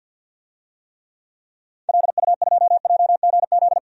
Morse code, heard as a keyed tone, about 48 wpm, 700 Hz. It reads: LU2JGZ
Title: DXCC-100 Call Signs - 6 Characters - 50wpm